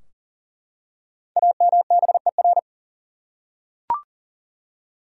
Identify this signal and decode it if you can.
Morse code, heard as a keyed tone, about 40 wpm, 700 Hz. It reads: AMBER
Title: USA Common Names - 40wpm